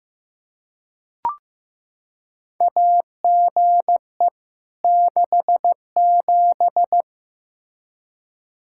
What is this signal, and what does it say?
Morse code, heard as a keyed tone, about 15 wpm, 700 Hz. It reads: AGE 67
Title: Practice QSO - Exchange 3 - More Info - 15wpm